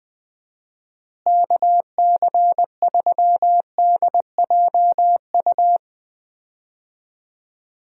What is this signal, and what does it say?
Morse code, heard as a keyed tone, about 20 wpm, 700 Hz. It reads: KC3DJU
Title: USA Call Signs - 6 Characters - Repeated 3x - 20wpm